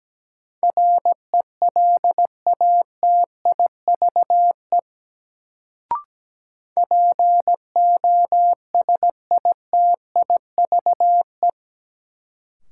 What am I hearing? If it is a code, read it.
Morse code, heard as a keyed tone, about 17 wpm, 700 Hz. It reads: RELATIVE POSITIVE
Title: Sentences from Top 1400 Words - New Word Drill Alt - 17wpm